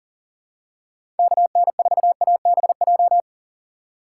Morse code, heard as a keyed tone, about 37 wpm, 700 Hz. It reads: KN4ABJ
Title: USA Call Signs - 6 Characters - 3x Character Spacing - 40wpm